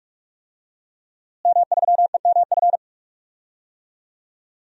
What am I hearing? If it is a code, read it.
Morse code, heard as a keyed tone, about 46 wpm, 700 Hz. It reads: M3EMF
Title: DXCC-100 Call Signs - 5 Characters - Repeated 3x - 45wpm